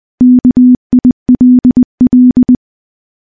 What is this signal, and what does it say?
Morse code, heard as a keyed tone, about 20 wpm, 250 Hz. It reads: KILL
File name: morse-training-20WPM